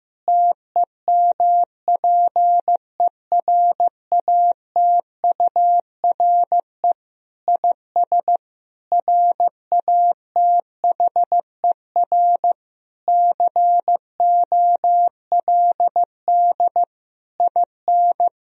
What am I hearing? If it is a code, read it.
Morse code, heard as a keyed tone, about 15 wpm, 700 Hz. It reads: TEMPERATURE IS RATHER COLD IN
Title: Sentences from Top 1000 Words - Review Part 4 - 15wpm